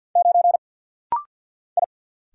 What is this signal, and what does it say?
Morse code, heard as a keyed tone, about 49 wpm, 700 Hz. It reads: 9 I
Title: Single Letter-Number Alt - Rapid-Fire 50wpm